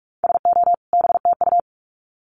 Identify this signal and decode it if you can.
Morse code, heard as a keyed tone, about 44 wpm, 700 Hz. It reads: SO BTU
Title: Common QSO Phrases 45wpm